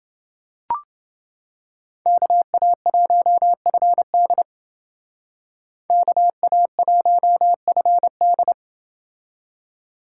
Morse code, heard as a keyed tone, about 28 wpm, 700 Hz. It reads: KA1FD KA1FD